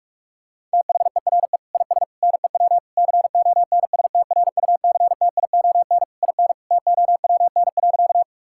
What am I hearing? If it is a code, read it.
Morse code, heard as a keyed tone, about 46 wpm, 700 Hz. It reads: THERE IS NEW CONSTRUCTION IN TOWN.
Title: Sentences from Top 1600 Words 45wpm